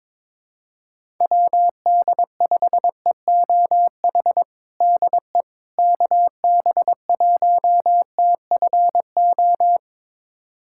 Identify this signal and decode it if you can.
Morse code, heard as a keyed tone, about 21 wpm, 700 Hz. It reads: WD5EOH DE KB1TFO
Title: USA Callsign DE Callsign - 6 Characters - 22wpm